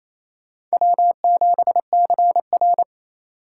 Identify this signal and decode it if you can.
Morse code, heard as a keyed tone, about 35 wpm, 700 Hz. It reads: W7CR
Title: Speed Racing - USA 1x2 Call Signs 28wpm